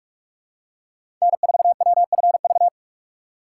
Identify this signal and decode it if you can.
Morse code, heard as a keyed tone, about 46 wpm, 700 Hz. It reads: N4WFV
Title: USA Call Signs - 5 Characters - 3x Character Spacing - 45wpm